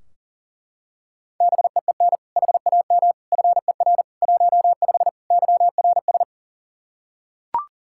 Morse code, heard as a keyed tone, about 40 wpm, 700 Hz. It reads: BEEN HAM FER 15 YRS
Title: K1USN SST QSO Practice 40wpm